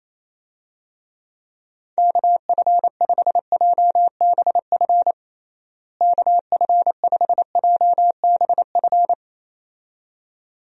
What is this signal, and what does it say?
Morse code, heard as a keyed tone, about 28 wpm, 700 Hz. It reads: KF5JBF KF5JBF
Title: USA Call Signs - 6 Characters - Repeated 2x - 28wpm